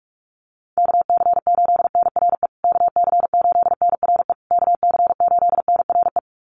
Morse code, heard as a keyed tone, about 44 wpm, 700 Hz. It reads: KC8NRE KC8NRE KC8NRE
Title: USA Call Signs Repeated 3x 45wpm